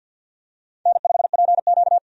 Morse code, heard as a keyed tone, about 50 wpm, 700 Hz. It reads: N5PQ
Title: USA Call Signs - 4 Characters - 50wpm